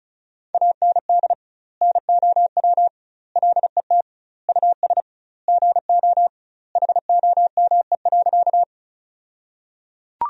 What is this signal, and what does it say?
Morse code, heard as a keyed tone, about 34 wpm, 700 Hz. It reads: AND NOW LET US GO HOME.
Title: 2x Word-Spacing - Sentences from Top 300 Words - 35wpm